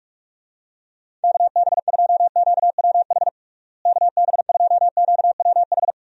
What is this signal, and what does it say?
Morse code, heard as a keyed tone, about 48 wpm, 700 Hz. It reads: KB2QWH KB2QWH
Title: USA Call Signs - 6 Characters - Repeated 3x - 45wpm